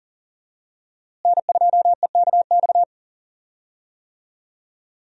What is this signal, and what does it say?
Morse code, heard as a keyed tone, about 40 wpm, 700 Hz. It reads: N2EKX
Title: USA Call Signs - 5 Characters - Repeated 2x - 40wpm